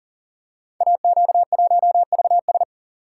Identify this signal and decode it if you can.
Morse code, heard as a keyed tone, about 38 wpm, 690 Hz. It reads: AQ1VS
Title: Code Groups - 5 Letter:Numbers - 40wpm